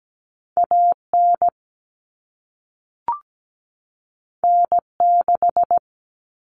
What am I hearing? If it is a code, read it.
Morse code, heard as a keyed tone, about 17 wpm, 700 Hz. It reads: AN N6
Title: Common Cut Numbers - 2-Digit 17wpm